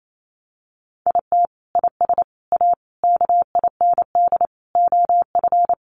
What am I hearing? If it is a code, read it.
Morse code, heard as a keyed tone, about 28 wpm, 700 Hz. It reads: IT IS A KIND OF
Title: Sentences from Top 1200 Words 28wpm